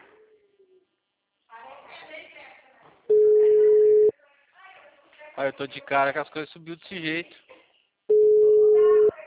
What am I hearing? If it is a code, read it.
Ringback tone (Europe, 425 Hz), with speech.